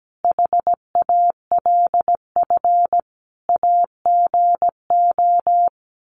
Morse code, heard as a keyed tone, about 17 wpm, 700 Hz. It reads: HALF AGO
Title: Sets of 5 Words Encore 17wpm